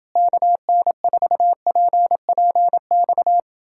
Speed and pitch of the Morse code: 29 wpm, 700 Hz

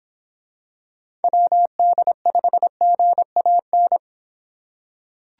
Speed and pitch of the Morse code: 24 wpm, 700 Hz

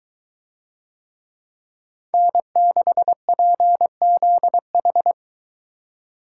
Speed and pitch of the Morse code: 22 wpm, 700 Hz

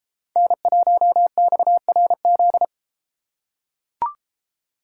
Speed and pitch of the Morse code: 32 wpm, 700 Hz

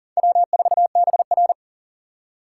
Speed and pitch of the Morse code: 51 wpm, 700 Hz

